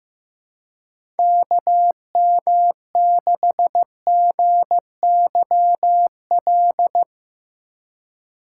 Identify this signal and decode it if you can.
Morse code, heard as a keyed tone, about 15 wpm, 700 Hz. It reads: KM6GYL
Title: USA Call Signs - 2x Character Spacing - 15wpm